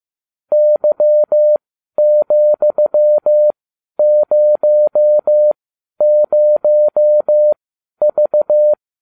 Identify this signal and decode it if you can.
Morse code, heard as a keyed tone, about 15 wpm, spaced as for 11 wpm, 600 Hz. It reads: Y,00V